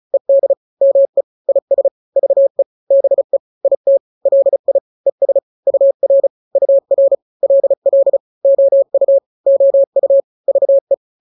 Morse code, heard as a keyed tone, about 34 wpm, 550 Hz. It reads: ED ME IS VE BE IT LI ES UR UR LL OU OU VE
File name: n-grams / 2-grams 35 30 WPM